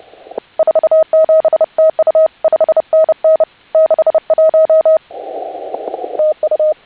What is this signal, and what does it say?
Morse code, heard as a keyed tone, about 29 wpm, 620 Hz. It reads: 47 TU 5NN 61 TU
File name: HA6A CW